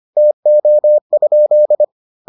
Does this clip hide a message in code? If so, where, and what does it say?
Yes: Morse code from 0.2 to 1.8 s, heard as a keyed tone, about 25 wpm, 600 Hz. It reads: TO?